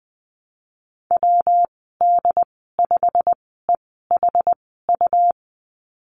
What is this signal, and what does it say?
Morse code, heard as a keyed tone, about 20 wpm, spaced as for 15 wpm, 700 Hz. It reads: WD5EHU
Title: 128 Lesson 21g - Number 6 - Call Signs 20wpm